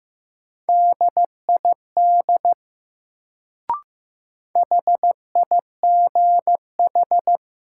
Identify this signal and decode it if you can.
Morse code, heard as a keyed tone, about 15 wpm, 700 Hz. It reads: DID HIGH
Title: Sets of 2 Words - New Word Drill - 15wpm